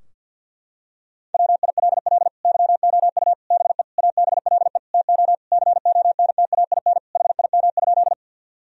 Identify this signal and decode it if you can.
Morse code, heard as a keyed tone, about 52 wpm, 700 Hz. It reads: WILL YOU BE ABLE TO CONTAIN HIM?
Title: Sentences from Top 1000 Words - Review Part 2 - 50wpm